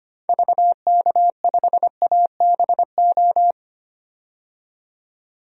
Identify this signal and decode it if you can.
Morse code, heard as a keyed tone, about 25 wpm, 700 Hz. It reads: VK5ABO
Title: DXCC-100 Call Signs - 6 Characters - 25wpm